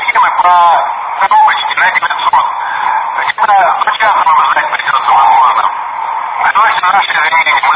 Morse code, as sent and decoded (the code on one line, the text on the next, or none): none